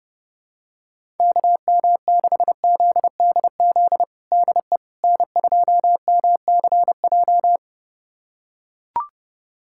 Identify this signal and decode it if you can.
Morse code, heard as a keyed tone, about 30 wpm, 700 Hz. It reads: KM6ZDZ DE N2MCJ
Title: USA Callsign DE Callsign 30wpm